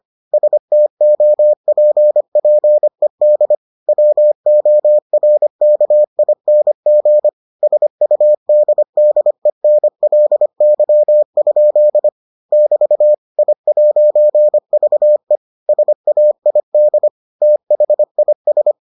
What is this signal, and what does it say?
Morse code, heard as a keyed tone, about 25 wpm, 600 Hz. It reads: STOPPED WORKING SUDDENLY? <BT> I'VE SAID THIS